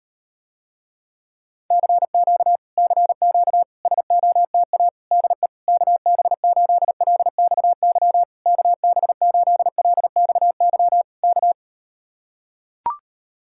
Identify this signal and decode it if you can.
Morse code, heard as a keyed tone, about 38 wpm, 700 Hz. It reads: CQ CQ SOTA DE KB8LXY KB8LXY K